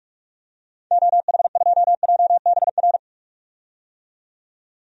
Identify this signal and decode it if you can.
Morse code, heard as a keyed tone, about 42 wpm, 700 Hz. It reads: OH2JBR
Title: DXCC-100 Call Signs - 6 Characters - 45wpm